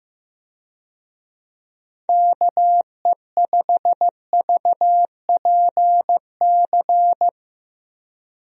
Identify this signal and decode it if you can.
Morse code, heard as a keyed tone, about 15 wpm, 700 Hz. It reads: KE5VPC
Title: USA Call Signs - 6 Characters - Repeated 3x - 15wpm